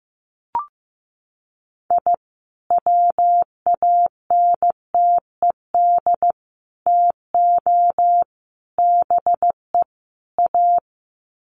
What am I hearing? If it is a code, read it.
Morse code, heard as a keyed tone, about 15 wpm, 700 Hz. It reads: I WANTED TO BE A
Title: Sets of 5 Words Encore2 15wpm